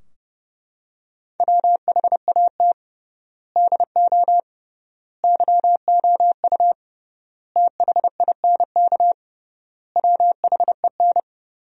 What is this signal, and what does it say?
Morse code, heard as a keyed tone, about 30 wpm, 700 Hz. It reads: WHAT DO YOU THINK WHEN
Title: 3x Word-Spacing - Sentences from Top 300 Words - 30wpm